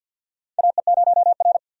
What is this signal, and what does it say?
Morse code, heard as a keyed tone, about 51 wpm, 700 Hz. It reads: AE0R
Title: USA Call Signs - 4 Characters - 50wpm